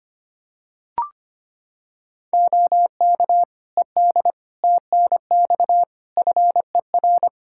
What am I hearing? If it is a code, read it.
Morse code, heard as a keyed tone, about 25 wpm, 700 Hz. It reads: OK ED TNX FER
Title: K1USN SST QSO Practice 25wpm